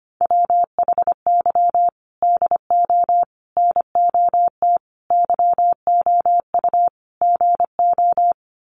Morse code, heard as a keyed tone, about 25 wpm, 700 Hz. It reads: WHY DO NOT YOU GO
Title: Sets of 5 Words Encore 25wpm